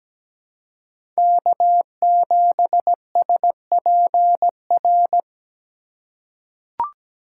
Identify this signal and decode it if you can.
Morse code, heard as a keyed tone, about 17 wpm, 700 Hz. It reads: K7SPR
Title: USA Call Signs - 5 Characters - 4x Character Spacing - 17wpm